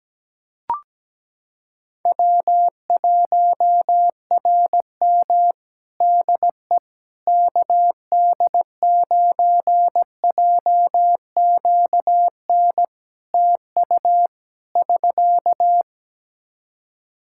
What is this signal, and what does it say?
Morse code, heard as a keyed tone, about 17 wpm, 700 Hz. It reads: W1RM DE KD9JQN TU <SK>
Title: K1USN SST QSO Practice 17wpm